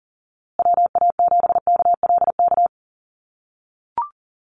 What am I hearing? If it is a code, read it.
Morse code, heard as a keyed tone, about 40 wpm, 700 Hz. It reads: WA7KLK
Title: USA Call Signs Repeated 3x 40wpm